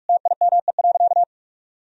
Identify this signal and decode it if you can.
Morse code, heard as a keyed tone, about 41 wpm, 700 Hz. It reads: TIME.
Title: Sentences from Top 1000 Words - Review Part 3 - 45wpm